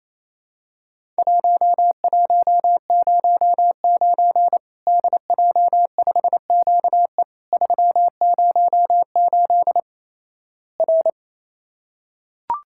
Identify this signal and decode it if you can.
Morse code, heard as a keyed tone, about 28 wpm, 700 Hz. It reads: 1109 DJ5QE 308 R
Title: Worked All Europe DX Contest - QTC Exchange - 28wpm